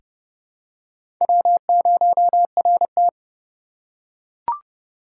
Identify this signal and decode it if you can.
Morse code, heard as a keyed tone, about 30 wpm, 700 Hz. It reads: W0RT
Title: USA Call Signs - 6x Character Spacing - 30wpm